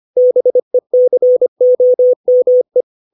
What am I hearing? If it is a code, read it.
Morse code, heard as a keyed tone, about 25 wpm, 500 Hz. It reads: BECOME
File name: 632 words / 632 Words 25WPM